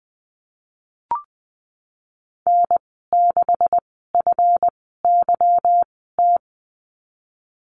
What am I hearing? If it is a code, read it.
Morse code, heard as a keyed tone, about 20 wpm, spaced as for 15 wpm, 700 Hz. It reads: N6FYT
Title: 154 Lesson 25g - Letter p - Call Signs 20wpm